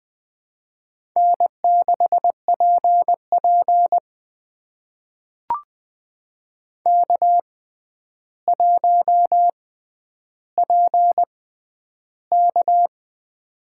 Morse code, heard as a keyed tone, about 20 wpm, spaced as for 10 wpm, 700 Hz. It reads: N6PP K1PK
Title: USA Call Signs - 4 Characters - 6x Character Spacing - 20wpm